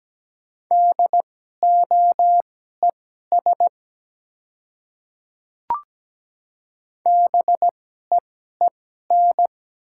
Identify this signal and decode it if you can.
Morse code, heard as a keyed tone, about 17 wpm, spaced as for 12 wpm, 700 Hz. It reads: DOES BEEN
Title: 2x Character Spacing - Top 300 Words - 17wpm